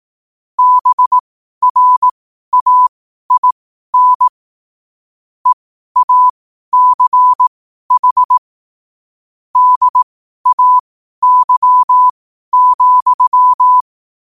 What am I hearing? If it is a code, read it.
Morse code, heard as a keyed tone, about 18 wpm, spaced as for 12 wpm, 1000 Hz. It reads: BRAIN EACH DAY,
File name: qotdfiles / 12wpm